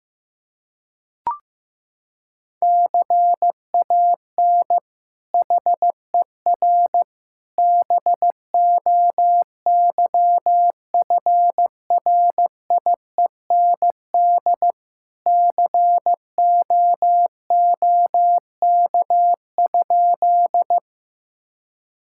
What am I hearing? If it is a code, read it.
Morse code, heard as a keyed tone, about 15 wpm, 700 Hz. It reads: CAN HER BOYFRIEND COOK?
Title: Sentences from Top 1000 Words - Review Part 5 - 15wpm